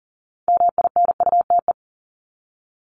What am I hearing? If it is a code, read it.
Morse code, heard as a keyed tone, about 39 wpm, 700 Hz. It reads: MINUTE
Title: Top 500 Words Repeated 40wpm